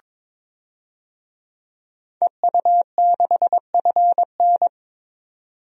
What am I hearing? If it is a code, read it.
Morse code, heard as a keyed tone, about 22 wpm, 700 Hz. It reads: EU6FN
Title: DXCC-100 Call Signs - 5 Characters - 22wpm